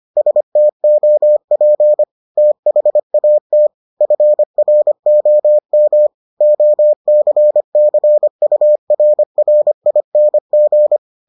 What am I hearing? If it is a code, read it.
Morse code, heard as a keyed tone, about 25 wpm, 600 Hz. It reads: STOP THAT FROM OCCURRING